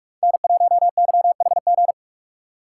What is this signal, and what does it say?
Morse code, heard as a keyed tone, about 47 wpm, 700 Hz. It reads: N1YHG
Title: USA Call Signs - 5 Characters - 4x Character Spacing - 45wpm